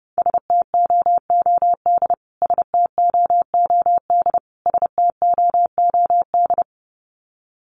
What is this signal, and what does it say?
Morse code, heard as a keyed tone, about 28 wpm, 700 Hz. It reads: STOOD STOOD STOOD
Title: Top 500 Words Repeated 30wpm